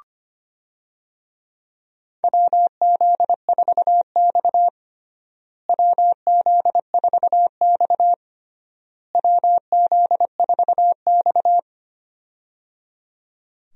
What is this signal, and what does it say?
Morse code, heard as a keyed tone, about 25 wpm, 700 Hz. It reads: WZ4X WZ4X WZ4X
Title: USA Call Signs - 4 Characters - Repeated 3x - 25wpm